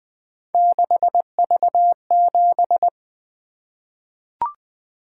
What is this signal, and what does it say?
Morse code, heard as a keyed tone, about 20 wpm, 700 Hz. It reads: 6V7
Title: Three Letter-Number Combinations - 20wpm